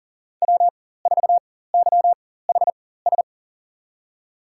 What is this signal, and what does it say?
Morse code, heard as a keyed tone, about 40 wpm, spaced as for 19 wpm, 700 Hz. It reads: W4YHS
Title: USA Call Signs - 4x Character Spacing - 40wpm